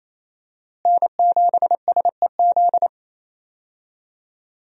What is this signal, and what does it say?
Morse code, heard as a keyed tone, about 28 wpm, 700 Hz. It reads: N7SEZ
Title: USA Call Signs - 5 Characters - Repeated 2x - 28wpm